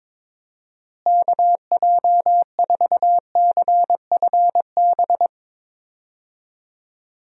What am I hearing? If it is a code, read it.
Morse code, heard as a keyed tone, about 22 wpm, 700 Hz. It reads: KJ4CFB
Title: USA Call Signs - 6 Characters - Repeated 2x - 22wpm